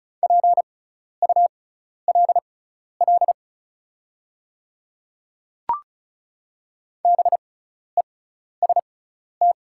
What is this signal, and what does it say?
Morse code, heard as a keyed tone, about 34 wpm, spaced as for 12 wpm, 700 Hz. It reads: PULL BEST
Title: 6x Character Spacing - Top 500 Words - 35wpm